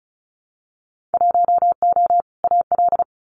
Morse code, heard as a keyed tone, about 36 wpm, 700 Hz. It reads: 1O AL
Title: Winter Field Day Exchanges 35wpm